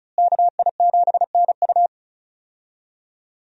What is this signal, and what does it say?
Morse code, heard as a keyed tone, about 36 wpm, 700 Hz. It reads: KI7NU
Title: USA Call Signs - 5 Characters - Repeated 3x - 35wpm